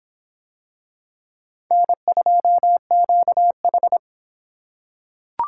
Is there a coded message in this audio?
Yes: Morse code from 1.7 to 4.0 s, heard as a keyed tone, about 25 wpm, 700 Hz. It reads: N2QH